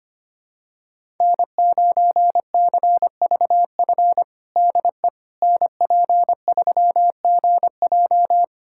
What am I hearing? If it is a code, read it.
Morse code, heard as a keyed tone, about 25 wpm, 700 Hz. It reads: N9CVF DE NP3GJ
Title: USA Callsign DE Callsign - 5 Characters - 25wpm